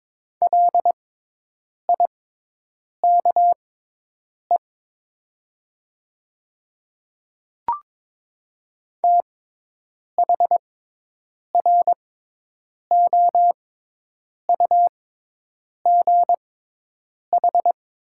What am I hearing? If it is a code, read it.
Morse code, heard as a keyed tone, about 22 wpm, spaced as for 8 wpm, 700 Hz. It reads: LIKE THROUGH
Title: 6x Character Spacing - Top 200 Words - 22wpm